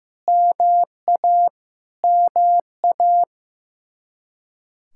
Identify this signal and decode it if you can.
Morse code, heard as a keyed tone, about 15 wpm, 700 Hz. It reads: MA MA
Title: Practice QSO - Exchange 1 - Beginning - 15wpm